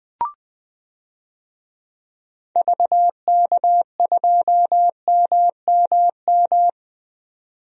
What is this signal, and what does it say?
Morse code, heard as a keyed tone, about 20 wpm, 700 Hz. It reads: VK2MMM
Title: DXCC-100 Call Signs - 6 Characters - 20wpm